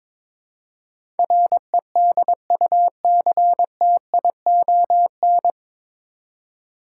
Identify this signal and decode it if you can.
Morse code, heard as a keyed tone, about 22 wpm, 700 Hz. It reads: REDUCTION
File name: 9 Letter Words - First 3 Letters Given - 22wpm